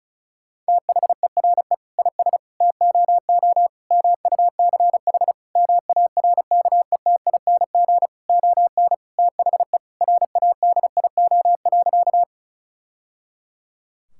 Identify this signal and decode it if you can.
Morse code, heard as a keyed tone, about 36 wpm, 700 Hz. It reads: THERE IS TOO MUCH MARKETING ON THE RADIO.
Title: Sentences from Top 1000 Words - Review Part 5 - 35wpm